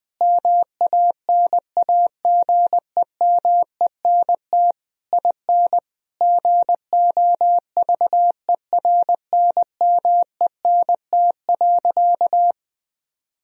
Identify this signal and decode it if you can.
Morse code, heard as a keyed tone, about 19 wpm, 700 Hz. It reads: MANAGEMENT IN GOVERNMENT.
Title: Sentences from Top 1000 Words - Review Part 4 - 20wpm